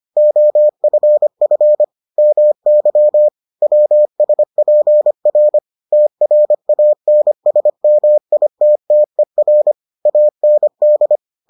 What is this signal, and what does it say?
Morse code, heard as a keyed tone, about 25 wpm, 600 Hz. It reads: OFF MY WSPR TRANSMITTER AND